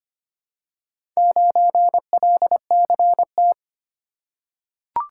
Morse code, heard as a keyed tone, about 25 wpm, 700 Hz. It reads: 9LCT
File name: practice / 25wpm / Four Letter-Number Combinations - 25wpm